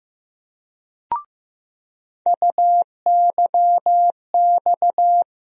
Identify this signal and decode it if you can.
Morse code, heard as a keyed tone, about 16 wpm, 700 Hz. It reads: UYX
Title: USA Call Sign Postfixes - 3 Letters 15wpm